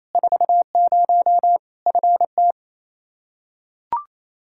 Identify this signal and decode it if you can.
Morse code, heard as a keyed tone, about 27 wpm, 700 Hz. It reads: 40 FT
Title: Practice QSO - Exchange 2 - Info - 28wpm